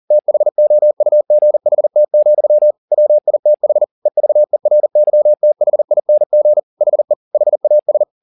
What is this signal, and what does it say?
Morse code, heard as a keyed tone, about 40 wpm, 600 Hz. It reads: THOUGHT, WITH EVERYTHING HE HAS